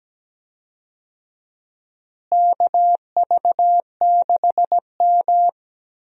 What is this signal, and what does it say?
Morse code, heard as a keyed tone, about 17 wpm, 700 Hz. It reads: KV6M
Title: USA Call Signs - 4 Characters - Repeated 2x - 17wpm